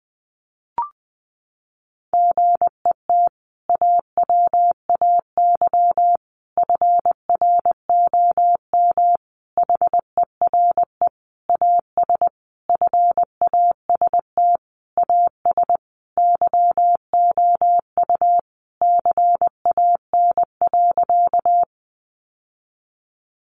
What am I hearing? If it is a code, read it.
Morse code, heard as a keyed tone, about 19 wpm, 700 Hz. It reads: GET AWAY FROM HERE AS FAST AS YOU CAN.
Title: Sentences from Top 600 Words 20wpm